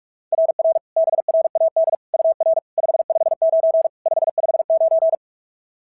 Morse code, heard as a keyed tone, about 44 wpm, 690 Hz. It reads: RR BRAD UR 559 559
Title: SOTA QSO Practice 45wpm